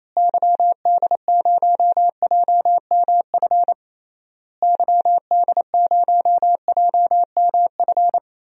Morse code, heard as a keyed tone, about 28 wpm, 700 Hz. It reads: YD0JMF YD0JMF
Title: DXCC-100 Call Signs - 6 Characters - Repeated 3x - 28wpm